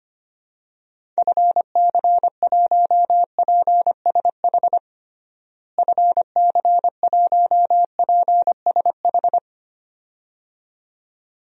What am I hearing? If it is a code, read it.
Morse code, heard as a keyed tone, about 25 wpm, 700 Hz. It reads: FC1PSH FC1PSH
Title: DXCC-100 Call Signs - 6 Characters - Repeated 2x - 25wpm